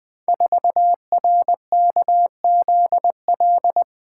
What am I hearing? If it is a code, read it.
Morse code, heard as a keyed tone, about 20 wpm, 700 Hz. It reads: 4RKZL
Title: Code Groups - 5 Letter:Numbers - 20wpm